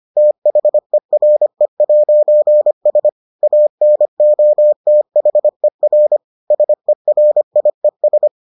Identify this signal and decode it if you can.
Morse code, heard as a keyed tone, about 25 wpm, 600 Hz. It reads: THERE'S ANOTHER SERIES